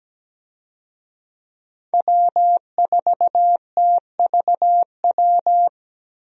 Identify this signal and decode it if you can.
Morse code, heard as a keyed tone, about 17 wpm, 700 Hz. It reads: W4TVW
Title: USA Call Signs - 5 Characters - Repeated 3x - 17wpm